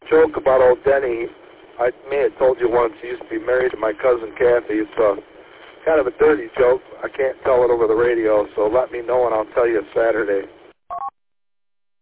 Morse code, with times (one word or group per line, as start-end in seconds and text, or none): none